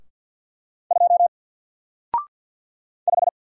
Morse code, heard as a keyed tone, about 48 wpm, 700 Hz. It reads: 2 5
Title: Single Letter-Number Alt 50wpm